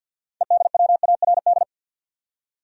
Morse code, heard as a keyed tone, about 49 wpm, 700 Hz. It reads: EDWARD
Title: USA Common Names - 50wpm